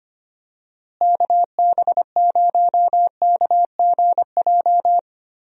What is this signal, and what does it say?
Morse code, heard as a keyed tone, about 25 wpm, 700 Hz. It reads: KB0KGJ
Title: USA Call Signs - 6 Characters - Repeated 3x - 25wpm